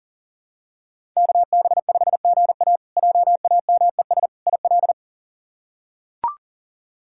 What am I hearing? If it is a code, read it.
Morse code, heard as a keyed tone, about 37 wpm, 700 Hz. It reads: KB5GA JAMES IL
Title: K1USN SST QSO Practice - 40wpm